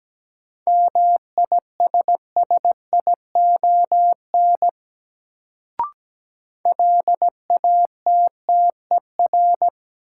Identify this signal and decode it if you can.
Morse code, heard as a keyed tone, about 17 wpm, 700 Hz. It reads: MISSION LATTER
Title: Sentences from Top 1600 Words - New Word Drill - 17wpm